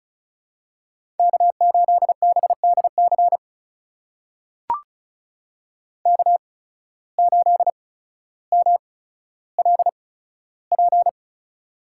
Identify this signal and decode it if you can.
Morse code, heard as a keyed tone, about 36 wpm, spaced as for 15 wpm, 700 Hz. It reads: K8BDC K8MLP
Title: USA Call Signs - 5 Characters - 8x Character Spacing - 35wpm